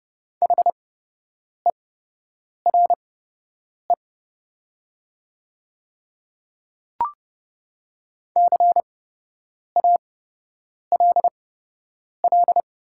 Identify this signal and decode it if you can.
Morse code, heard as a keyed tone, about 30 wpm, spaced as for 8 wpm, 700 Hz. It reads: HERE CALL
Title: 8x Character Spacing - Top 200 Words - 30wpm